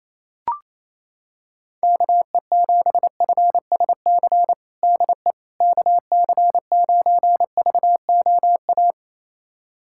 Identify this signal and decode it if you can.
Morse code, heard as a keyed tone, about 27 wpm, 700 Hz. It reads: KE7FSC DE KC9VOA
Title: USA Callsign DE Callsign - 6 Characters - 28wpm